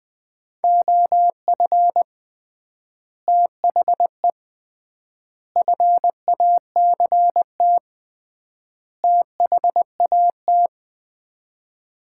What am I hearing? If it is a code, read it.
Morse code, heard as a keyed tone, about 21 wpm, 700 Hz. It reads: OF THE FACT THAT
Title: 3x Word-Spacing - Sets of 4 Words - 20wpm